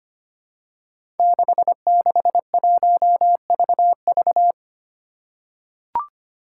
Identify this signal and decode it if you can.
Morse code, heard as a keyed tone, about 25 wpm, 700 Hz. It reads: 661VV
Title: Five Challenging Letter-Number Combinations 25wpm